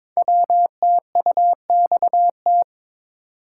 Morse code, heard as a keyed tone, about 22 wpm, 700 Hz. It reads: WTUXT
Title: Code Groups - 5 Letters - 22wpm